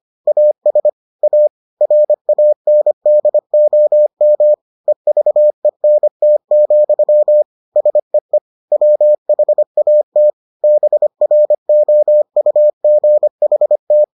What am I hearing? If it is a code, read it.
Morse code, heard as a keyed tone, about 25 wpm, 600 Hz. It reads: AS A RANDOM EVENT, SEE WHAT BROUGHT